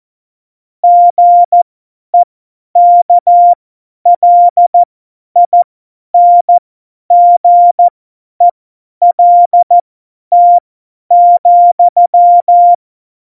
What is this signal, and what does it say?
Morse code, heard as a keyed tone, about 14 wpm, spaced as for 10 wpm, 700 Hz. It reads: GEKLINGELT,
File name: Morsekurs / Witze / Witz 14. Mai 2019, 50 F70